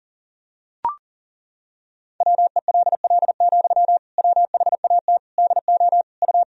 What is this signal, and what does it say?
Morse code, heard as a keyed tone, about 40 wpm, 700 Hz. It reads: WELL, WHAT DO U
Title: Sentences from Top 200 Words and Abbreviations 40wpm